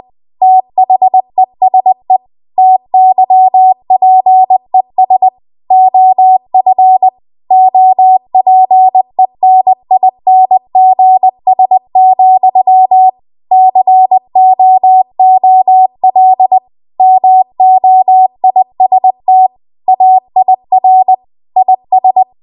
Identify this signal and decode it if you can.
Morse code, heard as a keyed tone, about 20 wpm, 750 Hz. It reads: THESE TYPES OF OPENINGS, COOL MOIST AIR IS
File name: Archive / 20 WPM / 20WPM